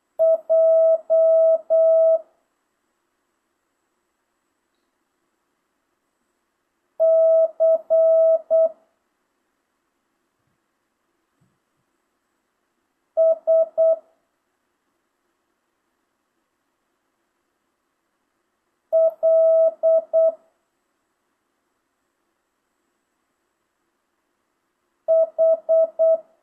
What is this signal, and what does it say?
Morse code, heard as a keyed tone, about 8 wpm, 650 Hz. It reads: J C S L H
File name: jh8cbh / 2025cw